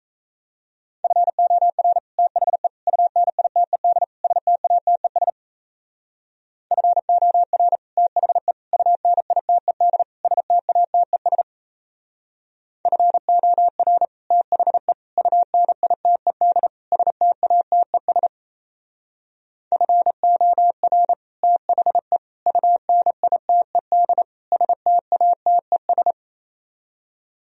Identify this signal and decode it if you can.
Morse code, heard as a keyed tone, about 35 wpm, 700 Hz. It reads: FOR THE UNITED STATES FOR THE UNITED STATES FOR THE UNITED STATES FOR THE UNITED STATES